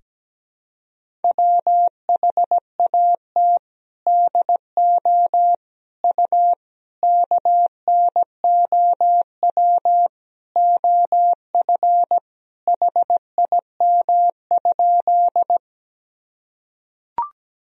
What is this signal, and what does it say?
Morse code, heard as a keyed tone, about 17 wpm, 700 Hz. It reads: WHAT DO U KNOW OF HIM?
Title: Sentences from Top 100 Words and Abbreviations 17wpm